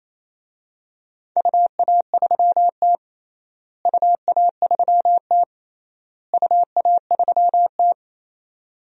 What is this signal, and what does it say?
Morse code, heard as a keyed tone, about 28 wpm, 700 Hz. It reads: UA3T UA3T UA3T
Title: DXCC-100 Call Signs - 4 Characters - Repeated 3x - 28wpm